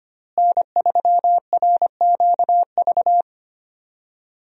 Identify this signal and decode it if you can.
Morse code, heard as a keyed tone, about 25 wpm, 700 Hz. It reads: N3RQV
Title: USA Call Signs - 5 Characters - Repeated 3x - 25wpm